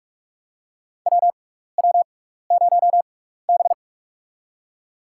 Morse code, heard as a keyed tone, about 46 wpm, spaced as for 15 wpm, 700 Hz. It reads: WW0B
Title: USA Call Signs - 4 Characters - 6x Character Spacing - 45wpm